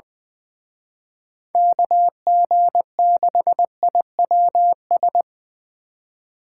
Morse code, heard as a keyed tone, about 23 wpm, 700 Hz. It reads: KG6IWS
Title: Speed Racing - USA Call Signs 20wpm